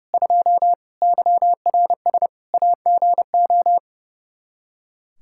Morse code, heard as a keyed tone, about 30 wpm, 700 Hz. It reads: 2 YRS AGO
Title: K1USN SST QSO Practice 30wpm